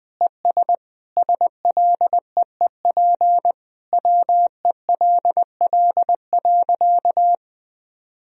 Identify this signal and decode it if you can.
Morse code, heard as a keyed tone, about 21 wpm, 700 Hz. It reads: ES SLEEP WELL.
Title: Sentences from Top 500 Words and Abbreviations 20wpm